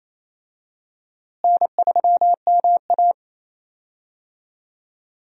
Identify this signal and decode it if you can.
Morse code, heard as a keyed tone, about 28 wpm, 700 Hz. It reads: N3MA
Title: USA Call Signs - 4 Characters - Repeated 3x - 28wpm